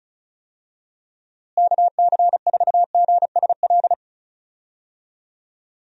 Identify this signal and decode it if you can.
Morse code, heard as a keyed tone, about 34 wpm, 700 Hz. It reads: KC4GSL
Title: USA Call Signs - 6 Characters - Repeated 2x - 35wpm